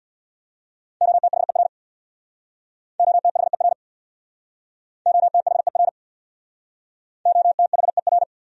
Morse code, heard as a keyed tone, about 67 wpm, 700 Hz. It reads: OTHER OTHER OTHER OTHER